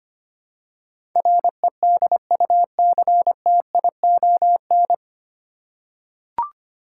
Morse code, heard as a keyed tone, about 25 wpm, 700 Hz. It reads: REDUCTION
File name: Nine Letter Words - 25wpm